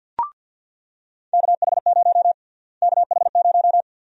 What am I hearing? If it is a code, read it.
Morse code, heard as a keyed tone, about 50 wpm, 700 Hz. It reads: KH0 KH0
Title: USA Call Sign Prefixes - Repeated 2x - 50wpm